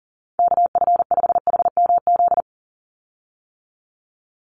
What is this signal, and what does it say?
Morse code, heard as a keyed tone, about 40 wpm, 700 Hz. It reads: KF5HMZ
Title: USA Call Signs 40wpm